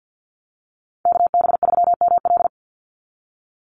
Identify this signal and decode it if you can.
Morse code, heard as a keyed tone, about 50 wpm, 700 Hz. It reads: KB3ML
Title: Call Signs Repeated 2x - 50wpm